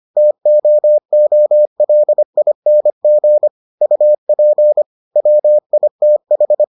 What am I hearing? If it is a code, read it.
Morse code, heard as a keyed tone, about 25 wpm, 600 Hz. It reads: TOOLING UP WITH